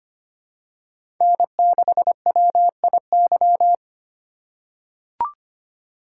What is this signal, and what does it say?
Morse code, heard as a keyed tone, about 25 wpm, 700 Hz. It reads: N6WIY
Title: USA Call Signs - 5 Characters - 4x Character Spacing - 25wpm